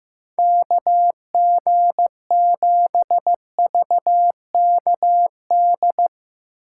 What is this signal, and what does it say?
Morse code, heard as a keyed tone, about 15 wpm, 700 Hz. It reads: KG7VKD
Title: USA Call Signs - 2x Character Spacing - 15wpm